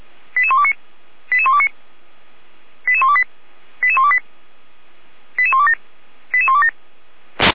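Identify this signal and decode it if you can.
Five-tone selective call (ZVEI), heard as quick runs of five tones, tones of about 80 ms. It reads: 88129 88129 88128 88128 88127 88127
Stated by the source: Five-tone ZVEI selective call received on SDR.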